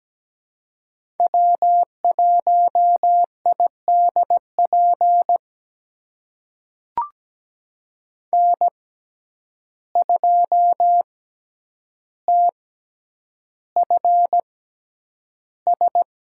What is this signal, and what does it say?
Morse code, heard as a keyed tone, about 17 wpm, spaced as for 9 wpm, 700 Hz. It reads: W1IDP N2TFS
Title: USA Call Signs - 5 Characters - 6x Character Spacing - 17wpm